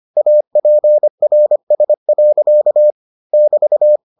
Morse code, heard as a keyed tone, about 25 wpm, 600 Hz. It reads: APRS. <BT>